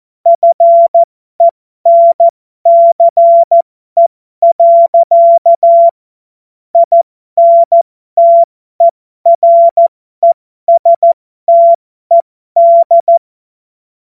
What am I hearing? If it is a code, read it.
Morse code, heard as a keyed tone, about 14 wpm, spaced as for 12 wpm, 680 Hz. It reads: FENCE. INTERESTED
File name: Morsekurs / Witze